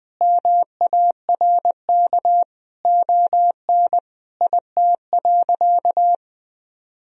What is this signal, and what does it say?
Morse code, heard as a keyed tone, about 20 wpm, 700 Hz. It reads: MARK ON IT.
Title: Sentences from Top 1000 Words - Review Part 1 - 20wpm